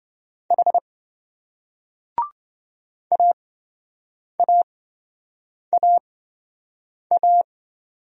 Morse code, heard as a keyed tone, about 28 wpm, spaced as for 7 wpm, 700 Hz. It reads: H AAAA